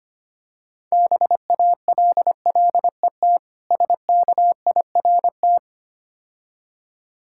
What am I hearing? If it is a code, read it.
Morse code, heard as a keyed tone, about 25 wpm, 700 Hz. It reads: BALLET SKIRT
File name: practice / 25wpm / Difficult Words - Speed-Racing with Hints - 25wpm